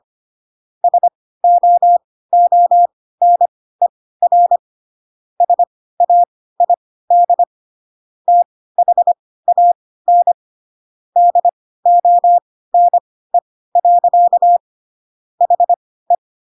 Morse code, heard as a keyed tone, about 25 wpm, spaced as for 16 wpm, 700 Hz. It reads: SOONER SAID THAN DONE. HE
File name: Pinocchio / Pinocchio ch5 10 25 ik1hge